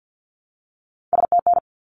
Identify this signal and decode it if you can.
Morse code, heard as a keyed tone, about 50 wpm, 700 Hz. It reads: STN
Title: Abbreviations 50wpm